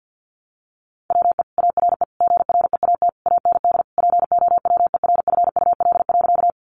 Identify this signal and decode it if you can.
Morse code, heard as a keyed tone, about 50 wpm, 700 Hz. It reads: WE ARE GREAT AND POWERFUL.
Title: Sentences from Top 1300 Words 50wpm